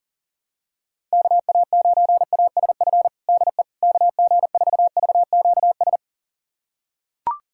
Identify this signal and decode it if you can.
Morse code, heard as a keyed tone, about 37 wpm, 700 Hz. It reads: KA9ASF DE KG4VQS
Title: USA Callsign DE Callsign 40wpm